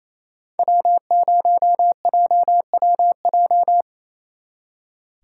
Morse code, heard as a keyed tone, about 34 wpm, 700 Hz. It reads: W0JWJ
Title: Speed Racing - USA 1x3 Call Signs 28wpm